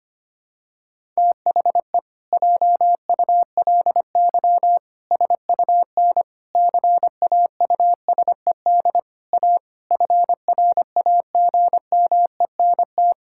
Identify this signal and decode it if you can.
Morse code, heard as a keyed tone, about 25 wpm, 700 Hz. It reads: THE JULY SUN CAUSED A FRAGMENT
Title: Pangram Challenge v2 - 25wpm